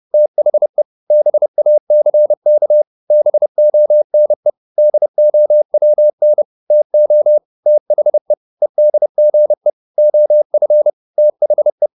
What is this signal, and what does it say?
Morse code, heard as a keyed tone, about 30 wpm, 600 Hz. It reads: THE BACK BONE DOWN TO THE EDGE OF THE
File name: the old man and the sea / the old man and the sea 1 30wpm